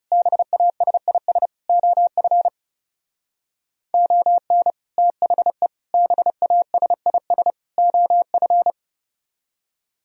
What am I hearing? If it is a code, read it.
Morse code, heard as a keyed tone, about 38 wpm, 690 Hz. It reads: BASIS OF ON THE BASIS OF